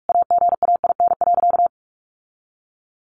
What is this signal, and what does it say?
Morse code, heard as a keyed tone, about 44 wpm, 700 Hz. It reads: AGAIN.
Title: Sentences from Top 1400 Words 45wpm